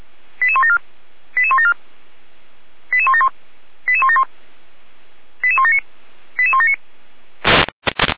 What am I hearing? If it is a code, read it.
Five-tone selective call (ZVEI), heard as quick runs of five tones, tones of about 70 ms. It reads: 88174 88174 88171 88171 88179 88179